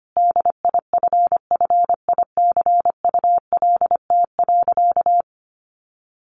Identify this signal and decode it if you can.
Morse code, heard as a keyed tone, about 25 wpm, 700 Hz. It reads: DIFFICULT.
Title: Sentences from Top 1300 Words 25wpm